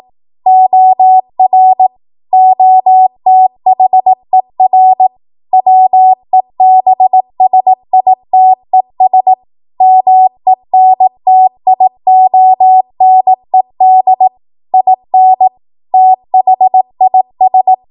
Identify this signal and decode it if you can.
Morse code, heard as a keyed tone, about 18 wpm, 750 Hz. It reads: OR OTHER WEBSITES MENTIONED IN THIS
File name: Morse / 18WPM